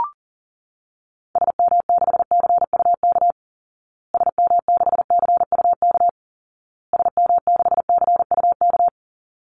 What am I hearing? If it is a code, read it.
Morse code, heard as a keyed tone, about 40 wpm, 700 Hz. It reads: SM6CUK SM6CUK SM6CUK
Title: Call Signs Repeated 2x - 40wpm